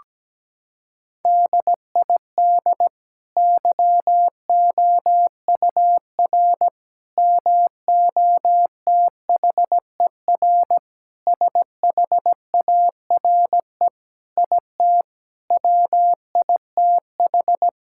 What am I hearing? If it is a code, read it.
Morse code, heard as a keyed tone, about 17 wpm, 700 Hz. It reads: DID YOUR MOTHER SHARE IT WITH
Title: Sentences from Top 1000 Words - Review Part 6 - 17wpm